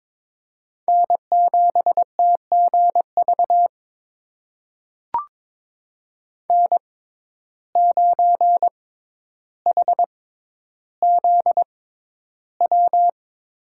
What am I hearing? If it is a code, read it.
Morse code, heard as a keyed tone, about 22 wpm, spaced as for 11 wpm, 700 Hz. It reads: N7TGV N9HZW
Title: USA Call Signs - 5 Characters - 6x Character Spacing - 22wpm